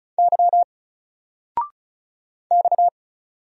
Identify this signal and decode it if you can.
Morse code, heard as a keyed tone, about 34 wpm, 700 Hz. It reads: Y X
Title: Single Letter-Number Alt 35wpm